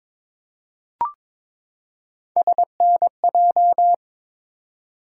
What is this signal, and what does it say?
Morse code, heard as a keyed tone, about 22 wpm, 700 Hz. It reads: SNJ
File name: ARRL Sections v2 - 22wpm